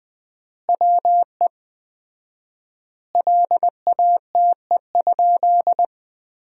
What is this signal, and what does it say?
Morse code, heard as a keyed tone, about 20 wpm, 700 Hz. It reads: WE LATE?
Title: 4x Word-Spacing - Sentences from Top 300 Words - 20wpm